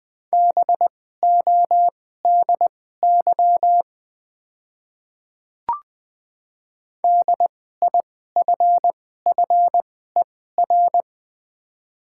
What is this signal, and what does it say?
Morse code, heard as a keyed tone, about 20 wpm, spaced as for 14 wpm, 700 Hz. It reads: BODY DIFFER
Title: 2x Character Spacing - Top 500 Words - 20wpm